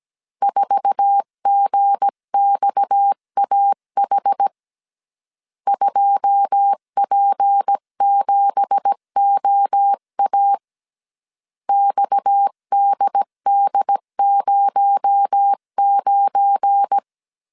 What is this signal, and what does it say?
Morse code, heard as a keyed tone, about 17 wpm, 800 Hz. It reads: 4GXAH 2P7OA XDD09
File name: CW / DET / 1 9 1500C S B 16wpm 2856